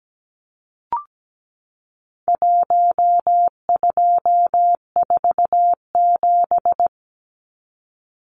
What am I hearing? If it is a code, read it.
Morse code, heard as a keyed tone, about 17 wpm, 700 Hz. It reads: 1247
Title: Numbers - 4-Digit 17wpm